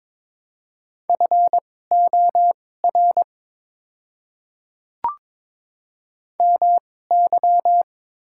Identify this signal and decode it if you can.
Morse code, heard as a keyed tone, about 22 wpm, spaced as for 16 wpm, 700 Hz. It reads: FOR MY
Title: 2x Character Spacing - Top 200 Words - 22wpm